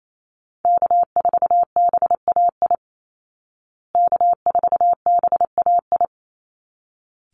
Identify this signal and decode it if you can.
Morse code, heard as a keyed tone, about 28 wpm, 700 Hz. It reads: K4BAI K4BAI
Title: Call Signs Repeated 3x - 28wpm